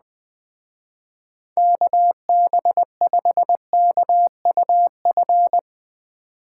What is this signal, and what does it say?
Morse code, heard as a keyed tone, about 20 wpm, 700 Hz. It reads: KB5KUF
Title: Speed Racing - USA 2x3 Call Signs 17wpm